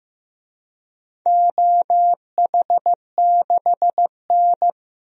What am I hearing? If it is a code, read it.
Morse code, heard as a keyed tone, about 15 wpm, 700 Hz. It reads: OH6N
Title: DXCC-100 Call Signs - 4 Characters - Repeated 3x - 15wpm